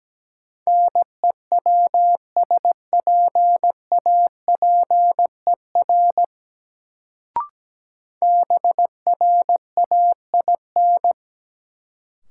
Morse code, heard as a keyed tone, about 17 wpm, 700 Hz. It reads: NEWSPAPER BRAIN
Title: Sentences from Top 1100 Words - New Word Drill Alt - 17wpm